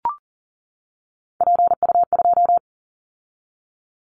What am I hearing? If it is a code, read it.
Morse code, heard as a keyed tone, about 37 wpm, 700 Hz. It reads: PU2
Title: Callsign Prefixes for Top 50 DXCC - 40wpm